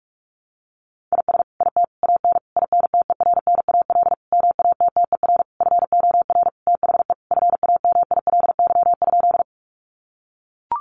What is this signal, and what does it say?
Morse code, heard as a keyed tone, about 44 wpm, 700 Hz. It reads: IS IT AN INTERNAL MATTER FOR THE FAMILY?
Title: Sentences from Top 1400 Words 45wpm